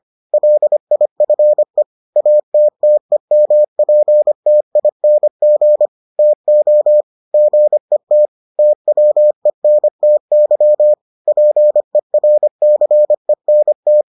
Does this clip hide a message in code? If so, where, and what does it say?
Yes: Morse code from 0.3 to 14.0 s, heard as a keyed tone, about 25 wpm, 600 Hz. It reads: LIFE ATTEMPTING TO GET TWENTY PERCENT